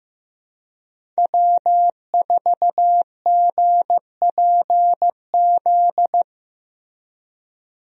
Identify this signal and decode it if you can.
Morse code, heard as a keyed tone, about 15 wpm, 700 Hz. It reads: W4GPZ
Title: USA Call Signs - 5 Characters - Repeated 3x - 15wpm